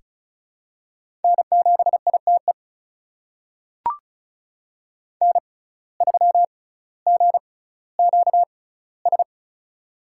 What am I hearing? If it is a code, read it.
Morse code, heard as a keyed tone, about 36 wpm, spaced as for 18 wpm, 700 Hz. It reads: N7ITE N3GQS
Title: USA Call Signs - 5 Characters - 6x Character Spacing - 35wpm